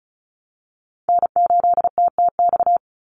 Morse code, heard as a keyed tone, about 34 wpm, 700 Hz. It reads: N8TTX
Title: USA Call Signs 35wpm